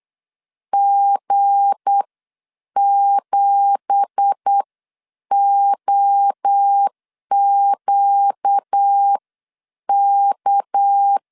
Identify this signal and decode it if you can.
Morse code, heard as a keyed tone, about 8 wpm, spaced as for 7 wpm, 800 Hz. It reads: G7OQK
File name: CW / DET / 1 9 1500C S C 8wpm 5653